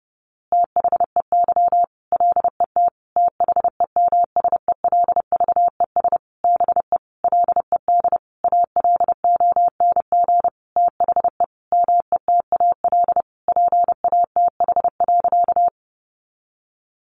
Morse code, heard as a keyed tone, about 30 wpm, 700 Hz. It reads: THEY LET THEMSELVES BE LED ALONG THE METAL PATH.
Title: A Sound Of Thunder 30wpm